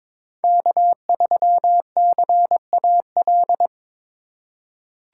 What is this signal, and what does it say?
Morse code, heard as a keyed tone, about 22 wpm, 700 Hz. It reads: K3CAL
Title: USA Call Signs - 3x Character Spacing - 22wpm